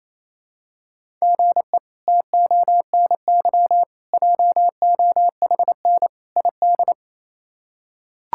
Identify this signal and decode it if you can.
Morse code, heard as a keyed tone, about 28 wpm, 700 Hz. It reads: GE TONY JOHN ID
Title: K1USN SST QSO Practice - 28wpm